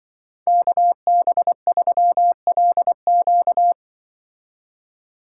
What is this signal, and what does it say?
Morse code, heard as a keyed tone, about 23 wpm, 700 Hz. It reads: KB3LQ
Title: Speed Racing - USA 2x2 Call Signs 20wpm